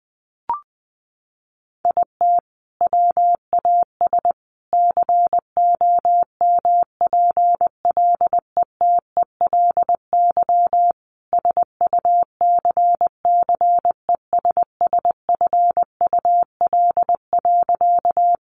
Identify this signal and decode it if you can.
Morse code, heard as a keyed tone, about 20 wpm, 700 Hz. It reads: IT WAS COMPLETELY SUCCESSFUL.
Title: Sentences from Top 1100 Words 20wpm